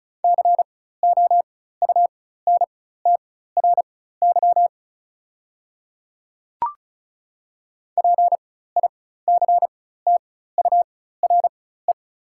Morse code, heard as a keyed tone, about 36 wpm, spaced as for 16 wpm, 690 Hz. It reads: COUNTRY PICTURE
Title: 4x Character Spacing - Top 300 Words - 35wpm